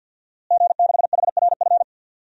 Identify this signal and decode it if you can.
Morse code, heard as a keyed tone, about 50 wpm, 700 Hz. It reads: G6HRF
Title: DXCC-100 Call Signs - 5 Characters - Repeated 2x - 50wpm